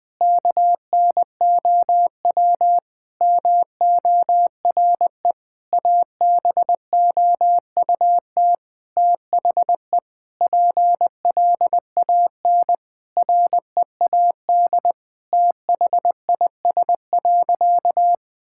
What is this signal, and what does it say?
Morse code, heard as a keyed tone, about 21 wpm, 700 Hz. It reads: KNOW MORE ABOUT THE PLAN READ THIS.
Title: Sentences with 8-Word Phrases - 20wpm